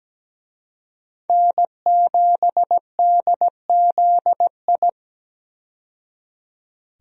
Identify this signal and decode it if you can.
Morse code, heard as a keyed tone, about 17 wpm, 700 Hz. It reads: N7DZI
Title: USA Call Signs - 5 Characters - 17wpm